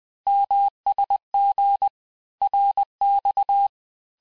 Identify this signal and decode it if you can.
Morse code, heard as a keyed tone, about 20 wpm, 800 Hz. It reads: MSG RX
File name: cours-du-lundi files / cours-1584-texte-abreviations-vitesse-19mpm-puis-24mpm